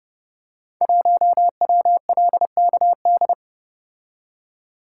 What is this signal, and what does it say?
Morse code, heard as a keyed tone, about 30 wpm, 700 Hz. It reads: 1WLKD
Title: Code Groups - 5 Letter:Numbers - 30wpm